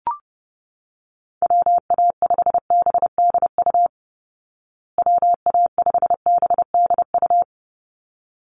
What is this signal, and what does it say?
Morse code, heard as a keyed tone, about 28 wpm, 700 Hz. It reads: WA5BDU WA5BDU
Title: Call Signs Repeated 3x - 30wpm